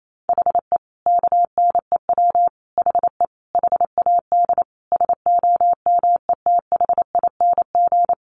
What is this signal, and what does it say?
Morse code, heard as a keyed tone, about 28 wpm, 700 Hz. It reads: HE KNEW HE HAD SOMETHING
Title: Sentences from Top 600 Words 28wpm